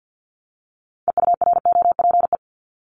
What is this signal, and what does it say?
Morse code, heard as a keyed tone, about 50 wpm, 700 Hz. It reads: EUROPE
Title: Multi-letter Phonemes Repeated 3x 50wpm